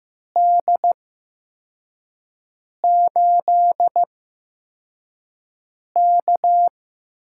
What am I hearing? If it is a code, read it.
Morse code, heard as a keyed tone, about 15 wpm, spaced as for 4 wpm, 700 Hz. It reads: D8K
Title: USA Call Signs - 4 Characters - 8x Character Spacing - 15wpm